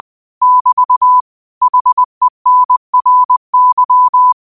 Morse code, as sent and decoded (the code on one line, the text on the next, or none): -...- / .... -. .-. -.--
<BT> HENRY